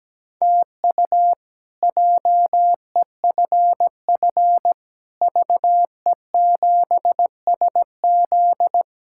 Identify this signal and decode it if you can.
Morse code, heard as a keyed tone, about 17 wpm, 700 Hz. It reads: TU JEFF VE7SZ
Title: K1USN SST QSO Practice - 17wpm